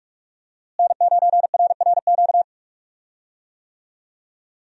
Morse code, heard as a keyed tone, about 48 wpm, 700 Hz. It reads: N9RRQ
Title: USA Call Signs - 5 Characters - Repeated 2x - 45wpm